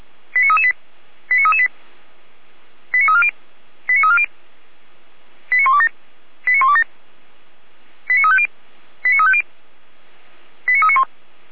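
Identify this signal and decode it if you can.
Five-tone selective call (ZVEI), heard as quick runs of five tones, tones of about 70 ms. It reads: 89338 89338 89340 89340 89127 89127 89350 89350 89391
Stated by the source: ZVEI five-tone selective call received with an SDR.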